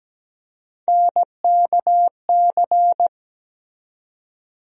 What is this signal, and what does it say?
Morse code, heard as a keyed tone, about 17 wpm, 700 Hz. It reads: NKC
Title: USA Call Sign Postfixes - 3 Letters 17wpm